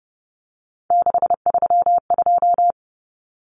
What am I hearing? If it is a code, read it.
Morse code, heard as a keyed tone, about 30 wpm, 700 Hz. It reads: 632
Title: Numbers - 3-Digit 30wpm